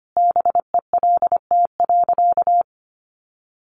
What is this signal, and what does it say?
Morse code, heard as a keyed tone, about 25 wpm, 700 Hz. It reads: BELT.
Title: Sentences from Top 1400 Words 25wpm